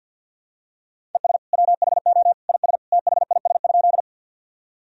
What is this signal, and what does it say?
Morse code, heard as a keyed tone, about 47 wpm, 700 Hz. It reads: ES WHO IS THIS?
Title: Sentences from Top 100 Words and Abbreviations 50wpm